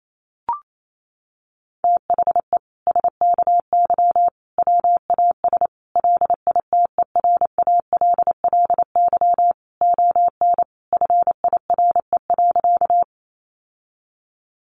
Morse code, heard as a keyed tone, about 28 wpm, 700 Hz. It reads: THE SKY WAS LITERALLY ON FIRE.
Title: Sentences from Top 1400 Words 28wpm